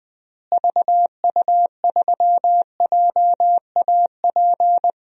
Morse code, heard as a keyed tone, about 20 wpm, 700 Hz. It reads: VU3JAP
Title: DXCC-100 Call Signs - 6 Characters - Repeated 3x - 20wpm